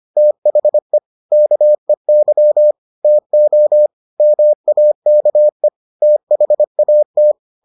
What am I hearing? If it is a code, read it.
Morse code, heard as a keyed tone, about 25 wpm, 600 Hz. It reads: THE KEY TO MAKE THAT